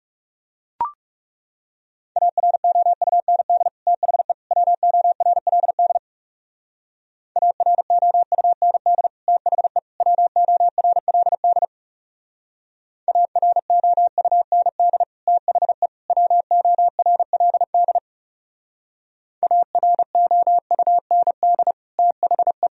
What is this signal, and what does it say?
Morse code, heard as a keyed tone, about 39 wpm, 700 Hz. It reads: AROUND THE WORLD AROUND THE WORLD AROUND THE WORLD AROUND THE